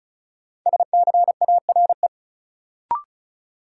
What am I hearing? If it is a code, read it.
Morse code, heard as a keyed tone, about 34 wpm, 700 Hz. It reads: SCARE
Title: Sentences from Top 1000 Words - Review Part 5 - New Word Drill - 35wpm